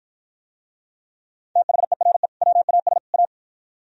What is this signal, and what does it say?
Morse code, heard as a keyed tone, about 51 wpm, 700 Hz. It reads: THERE WAS A